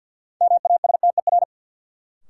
Morse code, heard as a keyed tone, about 48 wpm, 700 Hz. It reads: MASTER
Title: Sentences from Top 1300 Words - New Word Drill Alt - 50wpm